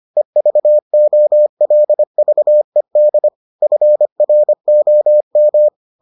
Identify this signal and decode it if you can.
Morse code, heard as a keyed tone, about 25 wpm, 600 Hz. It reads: EVOLVED FROM